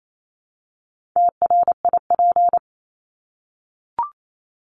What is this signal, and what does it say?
Morse code, heard as a keyed tone, about 28 wpm, 700 Hz. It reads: TRIP
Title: Top 1251-1500 Words 28wpm